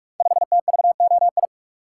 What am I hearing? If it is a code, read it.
Morse code, heard as a keyed tone, about 44 wpm, 700 Hz. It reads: 5TVOI
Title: Code Groups - 5 Letter:Numbers - 45wpm